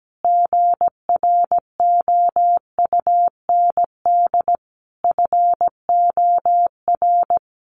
Morse code, heard as a keyed tone, about 17 wpm, 700 Hz. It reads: GROUND FOR
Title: Sentences from Top 1400 Words 17wpm